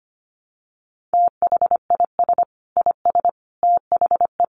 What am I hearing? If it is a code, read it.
Morse code, heard as a keyed tone, about 25 wpm, 700 Hz. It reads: THIS IS THE
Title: Sets of 5 Words Encore 25wpm